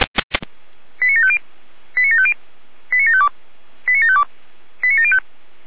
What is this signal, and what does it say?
Five-tone selective call (ZVEI), heard as quick runs of five tones, tones of about 70 ms. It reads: 89750 89750 89752 89752 89705